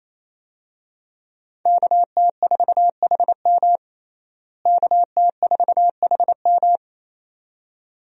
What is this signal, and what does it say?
Morse code, heard as a keyed tone, about 28 wpm, 700 Hz. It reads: KT4HM KT4HM
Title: USA Call Signs - 5 Characters - Repeated 2x - 28wpm